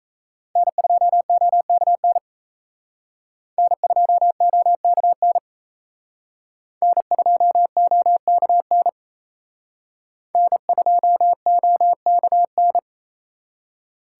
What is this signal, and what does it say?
Morse code, heard as a keyed tone, about 35 wpm, 700 Hz. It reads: N2OKN N2OKN N2OKN N2OKN